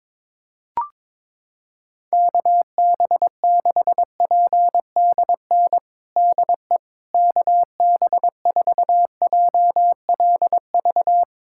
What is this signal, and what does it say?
Morse code, heard as a keyed tone, about 22 wpm, 700 Hz. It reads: KB6PDN DE KB4JLV
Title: USA Callsign DE Callsign - 6 Characters - 22wpm